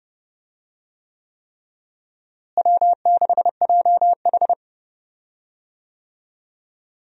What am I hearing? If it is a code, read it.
Morse code, heard as a keyed tone, about 32 wpm, 700 Hz. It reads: W6JH
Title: USA Call Signs - 4 Characters - Repeated 3x - 30wpm